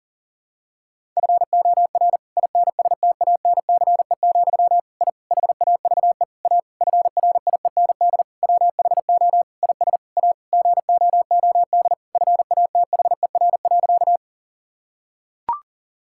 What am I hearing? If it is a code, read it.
Morse code, heard as a keyed tone, about 37 wpm, 700 Hz. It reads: FOR INSTANCE, I HAVE A FRIEND WHO IS A GOOD FATHER.
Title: Sentences from Top 1000 Words - Review Part 5 - 40wpm